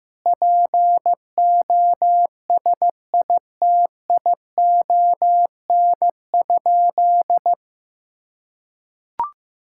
Morse code, heard as a keyed tone, about 16 wpm, 700 Hz. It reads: POSITION?
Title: Sentences from Top 1000 Words - Review Part 5 - 15wpm